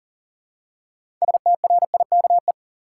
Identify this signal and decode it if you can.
Morse code, heard as a keyed tone, about 38 wpm, 700 Hz. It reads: STRIKE
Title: Six Letter Words - 40wpm